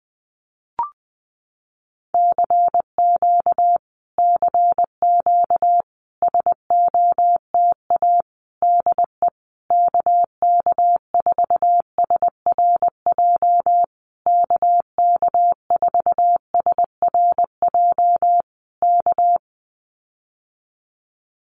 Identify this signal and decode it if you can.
Morse code, heard as a keyed tone, about 19 wpm, 700 Hz. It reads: CQ CQ SOTA DE KK4SRJ KK4SRJ K
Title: Common QSO Phrases Encore 20wpm